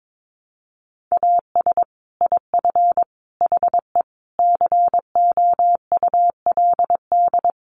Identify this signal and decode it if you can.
Morse code, heard as a keyed tone, about 22 wpm, 700 Hz. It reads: AS IF HE COULD
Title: Sentences from Top 1300 Words 22wpm